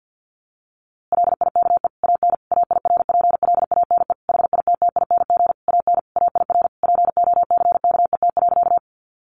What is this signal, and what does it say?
Morse code, heard as a keyed tone, about 50 wpm, 700 Hz. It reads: LIKE AN AIRPLANE HITTING AN AIR POCKET.
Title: A Sound Of Thunder 50wpm